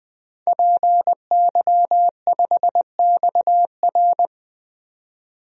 Morse code, heard as a keyed tone, about 20 wpm, 700 Hz. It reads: PY5XR
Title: DXCC-100 Call Signs - 5 Characters - 20wpm